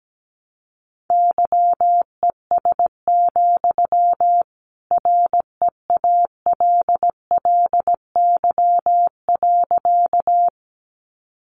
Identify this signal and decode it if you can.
Morse code, heard as a keyed tone, about 17 wpm, 700 Hz. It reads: YES, REALLY.
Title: Sentences from Top 600 Words 17wpm